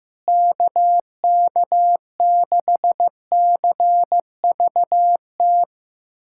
Morse code, heard as a keyed tone, about 15 wpm, 700 Hz. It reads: KK6CVT
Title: USA Callsign DE Callsign 15wpm